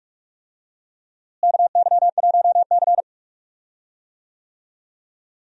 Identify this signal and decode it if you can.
Morse code, heard as a keyed tone, about 46 wpm, 700 Hz. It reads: KY1C
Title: USA Call Signs - 4 Characters - Repeated 2x - 45wpm